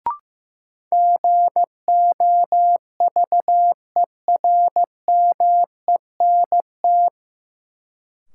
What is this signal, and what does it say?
Morse code, heard as a keyed tone, about 15 wpm, 700 Hz. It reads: GOVERMENT
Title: Sentences from Top 1300 Words - New Word Drill - 15wpm